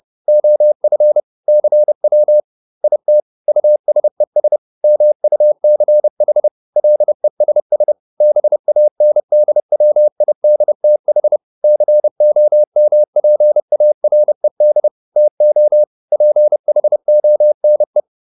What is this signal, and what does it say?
Morse code, heard as a keyed tone, about 30 wpm, 600 Hz. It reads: OF CW IT USES MUCH LESS BANDWIDTH COMPARED TO PHONE